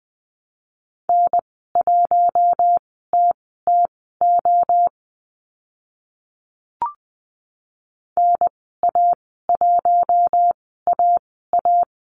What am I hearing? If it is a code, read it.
Morse code, heard as a keyed tone, about 20 wpm, spaced as for 14 wpm, 700 Hz. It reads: N1TTO NA1AA
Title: 042 Lesson 8g - Number 1 - Call Signs 20wpm